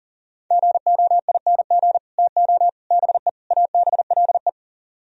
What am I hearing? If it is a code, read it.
Morse code, heard as a keyed tone, about 37 wpm, 700 Hz. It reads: GOING TO BE ABLE
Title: Speed-Racing - Sets of 4 Words 30wpm